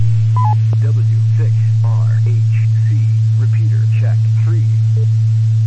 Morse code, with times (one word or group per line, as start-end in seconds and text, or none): none